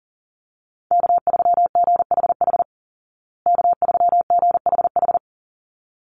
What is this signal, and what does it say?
Morse code, heard as a keyed tone, about 40 wpm, 700 Hz. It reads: K3GHH K3GHH
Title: Call Signs Repeated 3x - 40wpm